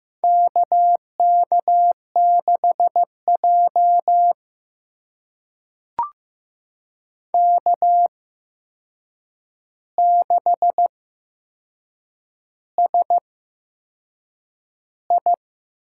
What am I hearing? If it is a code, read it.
Morse code, heard as a keyed tone, about 15 wpm, spaced as for 6 wpm, 700 Hz. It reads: KK6J K6SI
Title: USA Call Signs - 4 Characters - 8x Character Spacing - 15wpm